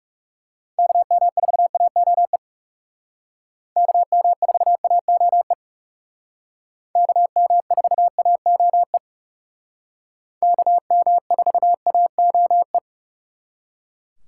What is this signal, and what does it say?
Morse code, heard as a keyed tone, about 38 wpm, 700 Hz. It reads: KM4AOE KM4AOE KM4AOE KM4AOE